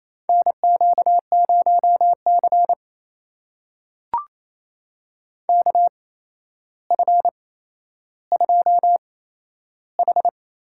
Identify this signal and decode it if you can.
Morse code, heard as a keyed tone, about 27 wpm, spaced as for 12 wpm, 700 Hz. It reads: NQ0C KF2H
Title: USA Call Signs - 4 Characters - 8x Character Spacing - 28wpm